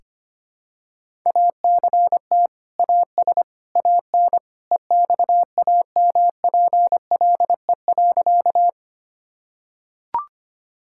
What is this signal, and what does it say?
Morse code, heard as a keyed tone, about 25 wpm, 700 Hz. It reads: ACT AS AN EXAMPLE.
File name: practice / 25wpm / Sentences with 2-Word Phrases - 25wpm